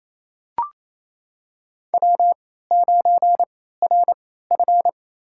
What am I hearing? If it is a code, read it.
Morse code, heard as a keyed tone, about 28 wpm, spaced as for 16 wpm, 700 Hz. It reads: W9RF
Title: USA Call Signs - 4 Characters - 3x Character Spacing - 28wpm